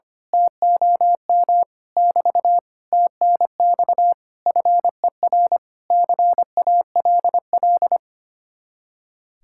Morse code, heard as a keyed tone, about 24 wpm, 700 Hz. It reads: TOM <BT> TNX FER CALL
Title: Speed-Racing Common QSO Phrases Encore - 25wpm